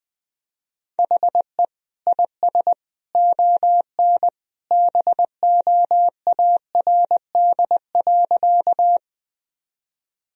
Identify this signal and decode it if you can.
Morse code, heard as a keyed tone, about 20 wpm, 700 Hz. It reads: HE IS ON BOARD.
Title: Sentences from Top 1000 Words - Review Part 4 - 20wpm